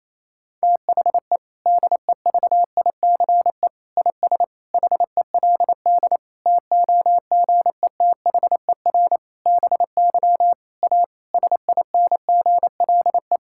Morse code, heard as a keyed tone, about 28 wpm, 700 Hz. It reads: THE DEVICE IS HELD TOGETHER BY A SINGLE
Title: Sentences from Top 1600 Words 28wpm